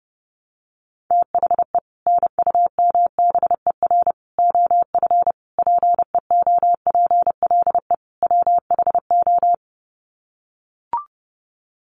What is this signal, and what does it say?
Morse code, heard as a keyed tone, about 30 wpm, 700 Hz. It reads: THE NUMBER OF PEOPLE WHO
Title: Sets of 5 Words Encore 30wpm